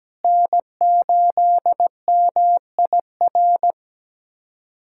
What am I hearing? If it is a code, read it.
Morse code, heard as a keyed tone, about 17 wpm, 700 Hz. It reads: N8MIR
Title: USA Callsign DE Callsign - 5 Characters - 17wpm